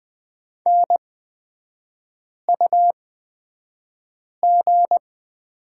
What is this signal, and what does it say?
Morse code, heard as a keyed tone, about 20 wpm, spaced as for 5 wpm, 700 Hz. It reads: NUG
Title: 236 Lesson 38e - Letter q - Three Characters 20wpm